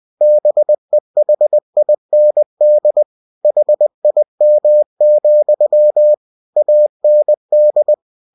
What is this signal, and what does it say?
Morse code, heard as a keyed tone, about 20 wpm, 600 Hz. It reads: BEHIND HIM, AND
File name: cw-books / winnie-the-pooh / winnie-20wpm-page-0044